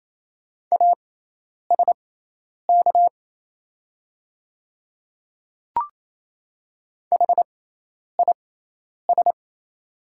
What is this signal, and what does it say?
Morse code, heard as a keyed tone, about 28 wpm, spaced as for 10 wpm, 690 Hz. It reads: ASK HIS
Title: 6x Character Spacing - Top 300 Words - 28wpm